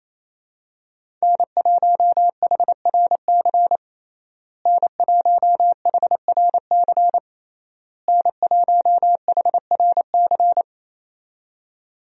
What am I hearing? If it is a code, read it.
Morse code, heard as a keyed tone, about 28 wpm, 700 Hz. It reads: N1HRC N1HRC N1HRC
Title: USA Call Signs - 5 Characters - Repeated 3x - 28wpm